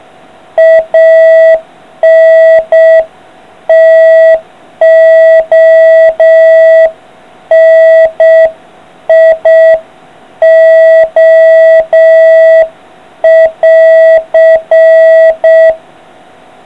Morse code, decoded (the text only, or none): ANTONIO<AR>